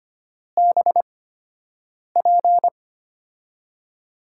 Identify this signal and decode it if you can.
Morse code, heard as a keyed tone, about 25 wpm, spaced as for 7 wpm, 700 Hz. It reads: BP
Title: USA Call Signs - 6 Characters - 8x Character Spacing - 25wpm